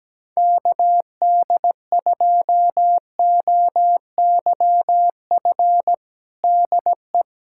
Morse code, heard as a keyed tone, about 17 wpm, 700 Hz. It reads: KD2OYF DE
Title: USA Callsign DE Callsign - 6 Characters - 17wpm